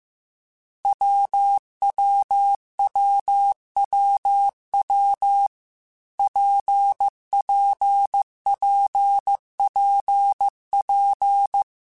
Morse code, heard as a keyed tone, about 15 wpm, 800 Hz. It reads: WWWWW PPPPP
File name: Difficult Combos / Difficult Combos-mp3s / WPJ WPJ1 @13